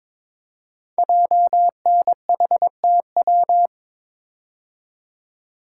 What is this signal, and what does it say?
Morse code, heard as a keyed tone, about 21 wpm, 700 Hz. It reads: JNHTW
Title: Code Groups - 5 Letters - 22wpm